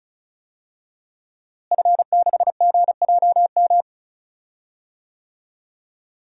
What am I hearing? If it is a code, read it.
Morse code, heard as a keyed tone, about 36 wpm, 700 Hz. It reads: F6GJM
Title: DXCC-100 Call Signs - 5 Characters - Repeated 3x - 35wpm